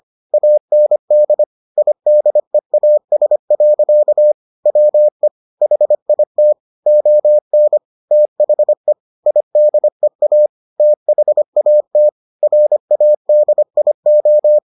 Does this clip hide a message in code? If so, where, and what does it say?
Yes: Morse code from 0.3 to 14.6 s, heard as a keyed tone, about 25 wpm, 600 Hz. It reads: AND IDEAS. WE HIT ON THE IDEA THAT RADIO